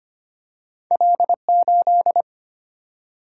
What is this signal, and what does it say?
Morse code, heard as a keyed tone, about 25 wpm, 700 Hz. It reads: L8
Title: Single-Letter Single-Number 25wpm